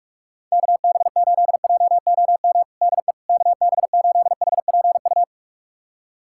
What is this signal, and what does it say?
Morse code, heard as a keyed tone, about 42 wpm, 700 Hz. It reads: KB8JOM DE KB8HPU
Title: USA Callsign DE Callsign - 6 Characters - 45wpm